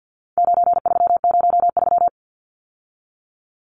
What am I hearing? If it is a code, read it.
Morse code, heard as a keyed tone, about 50 wpm, 700 Hz. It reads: 9303
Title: Numbers - 4-Digit 50wpm